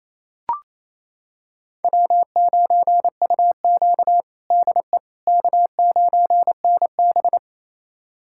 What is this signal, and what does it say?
Morse code, heard as a keyed tone, about 28 wpm, 700 Hz. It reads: W9UQ DE K9NB
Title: USA Callsign DE Callsign - 4 Characters - 28wpm